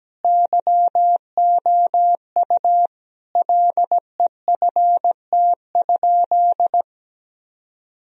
Morse code, heard as a keyed tone, about 17 wpm, 700 Hz. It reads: YOU LEFT?
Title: Sentences from Top 1000 Words - Review Part 2 - 17wpm